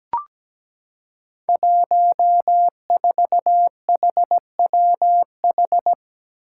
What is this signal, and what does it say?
Morse code, heard as a keyed tone, about 17 wpm, 700 Hz. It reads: 14HWH
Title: Five Challenging Letter-Number Combinations 17wpm